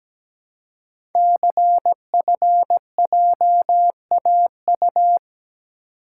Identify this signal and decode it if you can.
Morse code, heard as a keyed tone, about 17 wpm, 700 Hz. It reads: CFJAU
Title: Code Groups - 5 Letters - 17wpm